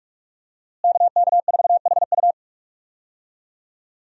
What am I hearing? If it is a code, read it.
Morse code, heard as a keyed tone, about 46 wpm, 700 Hz. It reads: KK4HU
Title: USA Call Signs - 5 Characters - Repeated 3x - 45wpm